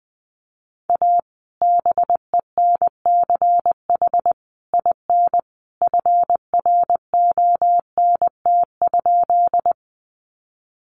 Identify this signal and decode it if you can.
Morse code, heard as a keyed tone, about 20 wpm, 700 Hz. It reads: A BENCH IN FRONT?
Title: Sentences from Top 1400 Words 20wpm